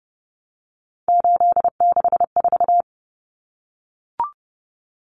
Morse code, heard as a keyed tone, about 30 wpm, 700 Hz. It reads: 864
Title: Common Cut Numbers - 3-Digit 30wpm